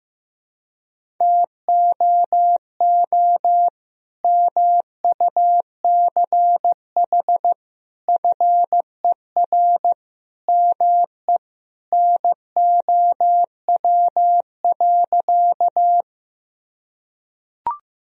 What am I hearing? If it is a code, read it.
Morse code, heard as a keyed tone, about 16 wpm, 700 Hz. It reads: TOO MUCH FER ME NOW.
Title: Sentences from Top 200 Words and Abbreviations 15wpm